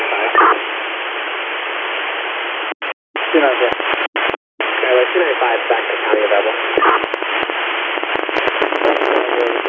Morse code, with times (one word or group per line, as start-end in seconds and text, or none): none